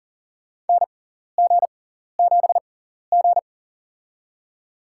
Morse code, heard as a keyed tone, about 40 wpm, spaced as for 14 wpm, 700 Hz. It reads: NG7G
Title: USA Call Signs - 4 Characters - 6x Character Spacing - 40wpm